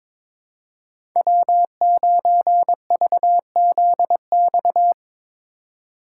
Morse code, heard as a keyed tone, about 22 wpm, 700 Hz. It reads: W9VZX
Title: USA Call Signs - 3x Character Spacing - 22wpm